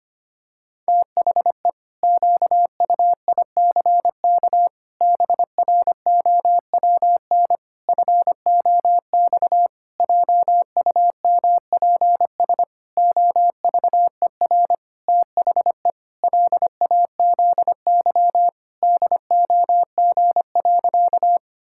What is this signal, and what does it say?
Morse code, heard as a keyed tone, about 25 wpm, 700 Hz. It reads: THE QUICK BROWN FOX JUMPS OVER THE LAZY DOG.
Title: Pangram Challenge v2 - 25wpm